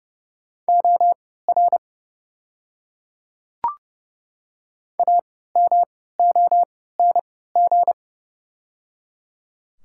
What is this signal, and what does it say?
Morse code, heard as a keyed tone, about 30 wpm, spaced as for 17 wpm, 690 Hz. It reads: OR AMONG
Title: 3x Character Spacing - Top 500 Words - 30wpm